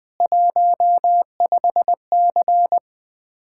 Morse code, heard as a keyed tone, about 20 wpm, 700 Hz. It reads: 15C
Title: Common QSO Phrases - 4x Word-Spacing - 20wpm